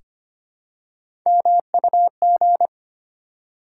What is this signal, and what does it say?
Morse code, heard as a keyed tone, about 25 wpm, 700 Hz. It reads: MUG
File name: practice / 25wpm / Difficult Words with Hints - 25wpm